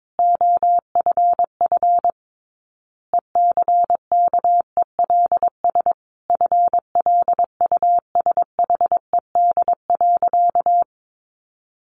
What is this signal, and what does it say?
Morse code, heard as a keyed tone, about 22 wpm, 700 Hz. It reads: OFF ECKELS FLUSHED.
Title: A Sound Of Thunder 22wpm